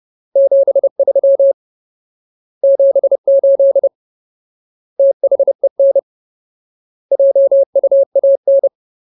Morse code, heard as a keyed tone, about 30 wpm, 550 Hz. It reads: 73 78 THEN JUAN
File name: Ultimate Word List / 30WPM Ultimate